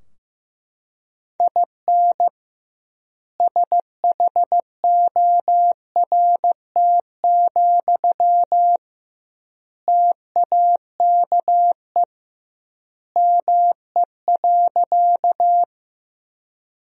Morse code, heard as a keyed tone, about 15 wpm, 700 Hz. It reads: IN SHORT, TAKE ME.
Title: 2x Word-Spacing - Sentences from Top 400 Words - 15wpm